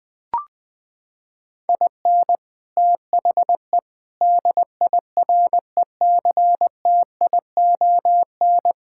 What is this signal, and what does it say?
Morse code, heard as a keyed tone, about 19 wpm, 700 Hz. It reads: IN THE DIRECTION
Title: Sentences with 4-Word Phrases - 20wpm